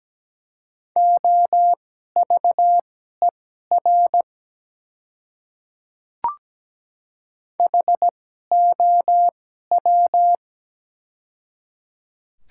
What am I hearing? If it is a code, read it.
Morse code, heard as a keyed tone, about 17 wpm, spaced as for 12 wpm, 690 Hz. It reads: OVER HOW
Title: 2x Character Spacing - Top 100 Words - 17wpm